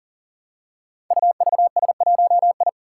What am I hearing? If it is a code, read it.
Morse code, heard as a keyed tone, about 41 wpm, 700 Hz. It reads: UVS1I
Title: Five Challenging Letter-Number Combinations 40wpm